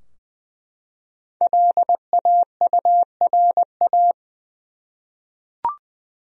Morse code, heard as a keyed tone, about 20 wpm, 700 Hz. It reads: LAURA
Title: USA Common Names - 20wpm